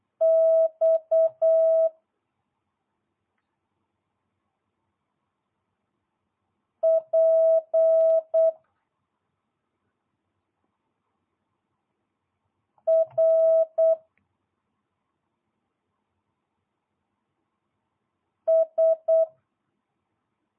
Morse code, heard as a keyed tone, about 8 wpm, 650 Hz. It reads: X P R S